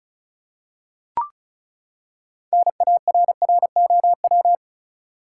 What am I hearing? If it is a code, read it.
Morse code, heard as a keyed tone, about 36 wpm, 700 Hz. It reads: NARROW
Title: Six Letter Words - 35wpm